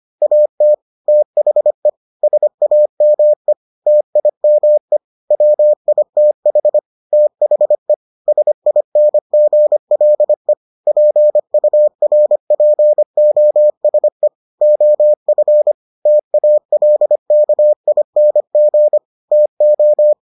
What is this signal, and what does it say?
Morse code, heard as a keyed tone, about 25 wpm, 600 Hz. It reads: AT THE SAME TIME WITH THE SINGLE PURPOSE OF TALKING TO